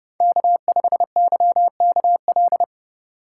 Morse code, heard as a keyed tone, about 30 wpm, 700 Hz. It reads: K5YKL
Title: USA Call Signs - 5 Characters - 4x Character Spacing - 30wpm